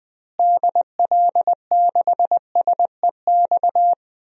Morse code, heard as a keyed tone, about 20 wpm, 700 Hz. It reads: DL6SEX
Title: DXCC-100 Call Signs - 6 Characters - Repeated 2x - 20wpm